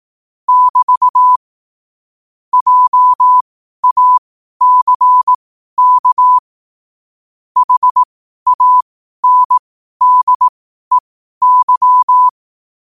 Morse code, heard as a keyed tone, about 18 wpm, spaced as for 12 wpm, 1000 Hz. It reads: <BT> JACK HANDEY
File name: qotdfiles / 12wpm